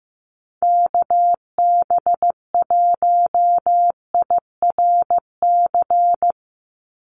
Morse code, heard as a keyed tone, about 15 wpm, 700 Hz. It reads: KB1IRC
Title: USA Call Signs 15wpm